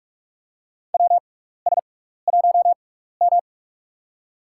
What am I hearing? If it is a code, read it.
Morse code, heard as a keyed tone, about 47 wpm, spaced as for 16 wpm, 700 Hz. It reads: WS1M
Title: USA Call Signs - 4 Characters - 6x Character Spacing - 45wpm